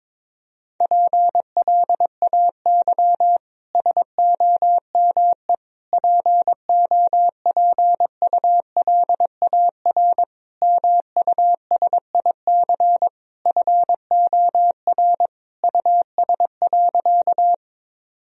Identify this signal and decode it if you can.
Morse code, heard as a keyed tone, about 22 wpm, 700 Hz. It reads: PLAY SOME POPULAR MUSIC FOR US.
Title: Sentences from Top 1000 Words - Review Part 3 - 22wpm